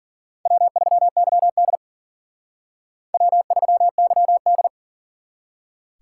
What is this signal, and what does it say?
Morse code, heard as a keyed tone, about 52 wpm, 700 Hz. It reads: W3YD W3YD